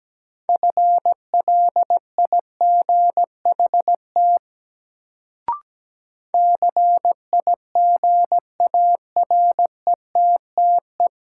Morse code, heard as a keyed tone, about 17 wpm, 700 Hz. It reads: FLIGHT CIGARETTE
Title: Sentences from Top 1000 Words - New Word Drill - 17wpm